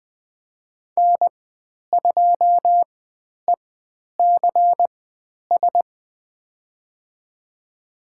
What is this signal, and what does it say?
Morse code, heard as a keyed tone, about 20 wpm, spaced as for 10 wpm, 700 Hz. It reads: N2ECS
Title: 109 Lesson 18g - Letter m - Call Signs 20wpm